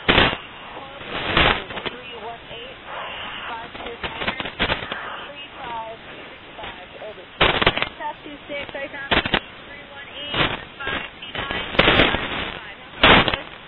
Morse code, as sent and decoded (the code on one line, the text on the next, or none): none